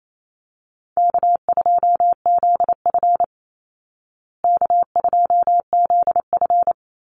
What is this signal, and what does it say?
Morse code, heard as a keyed tone, about 28 wpm, 700 Hz. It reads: K2ZF K2ZF
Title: Easy Call Signs Repeated 3x - 28wpm